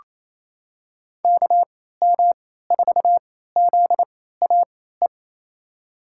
Morse code, heard as a keyed tone, about 27 wpm, spaced as for 16 wpm, 700 Hz. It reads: KM4ZAE
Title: USA Call Signs - 3x Character Spacing - 28wpm